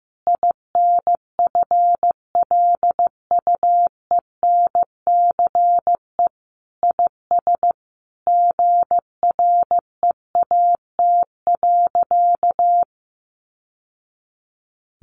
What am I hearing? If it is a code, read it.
Morse code, heard as a keyed tone, about 15 wpm, 700 Hz. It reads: INFLUENCE IS GREAT.
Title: Sentences from Top 1100 Words 15wpm